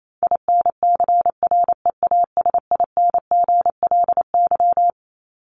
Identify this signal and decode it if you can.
Morse code, heard as a keyed tone, about 28 wpm, 700 Hz. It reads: INCREASINGLY
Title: LY Suffix - Top 100 Words with Base Words 28wpm